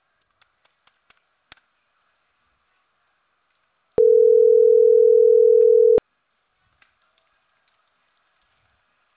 Ringback tone (North America), 440 + 480 Hz.